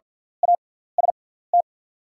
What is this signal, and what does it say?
Morse code, heard as a keyed tone, about 48 wpm, spaced as for 17 wpm, 690 Hz. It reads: AST
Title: 6x Character Spacing - Top 300 Words - 50wpm